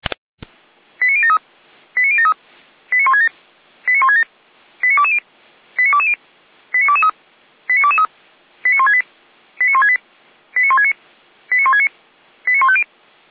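Five-tone selective call (ZVEI), heard as quick runs of five tones, tones of about 70 ms. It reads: 89073 89073 89167 89167 89220 89220 89223 89223 89168 89168 89169 89169 89150
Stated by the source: ZVEI five-tone selective call received on SDR, from the Feuerwehr Orlamünde audio server.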